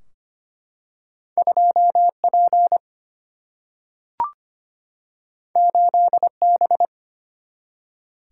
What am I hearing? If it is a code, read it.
Morse code, heard as a keyed tone, about 25 wpm, 700 Hz. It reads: 2P 8B
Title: Single-Letter Single-Number 25wpm